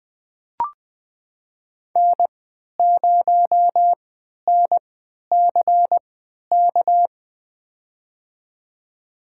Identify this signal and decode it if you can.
Morse code, heard as a keyed tone, about 20 wpm, spaced as for 11 wpm, 700 Hz. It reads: N0NCK
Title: USA Call Signs - 5 Characters - 3x Character Spacing - 20wpm